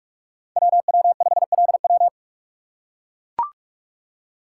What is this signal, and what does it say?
Morse code, heard as a keyed tone, about 44 wpm, 690 Hz. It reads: WW5LW
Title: USA Call Signs - 5 Characters - 8x Character Spacing - 45wpm